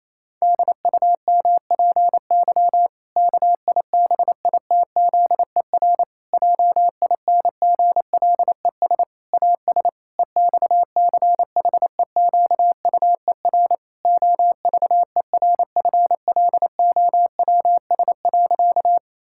Morse code, heard as a keyed tone, about 28 wpm, 700 Hz. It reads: DUMPY KIBITZER JINGLES AS EXCHEQUER OVERFLOWS.
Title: Pangram Challenge v2 - 28wpm